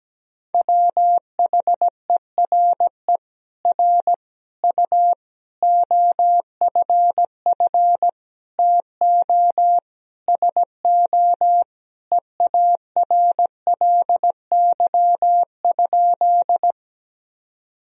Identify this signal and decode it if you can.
Morse code, heard as a keyed tone, about 17 wpm, 700 Hz. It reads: WHERE R U OFF TO SO EARLY?
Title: Sentences from Top 400 Words and Abbreviations 17wpm